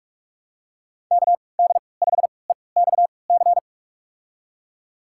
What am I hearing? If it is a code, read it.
Morse code, heard as a keyed tone, about 46 wpm, spaced as for 26 wpm, 690 Hz. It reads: KD5EXC
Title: USA Call Signs - 6 Characters - 3x Character Spacing - 45wpm